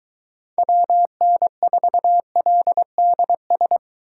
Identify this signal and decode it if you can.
Morse code, heard as a keyed tone, about 22 wpm, 700 Hz. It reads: WN4LDS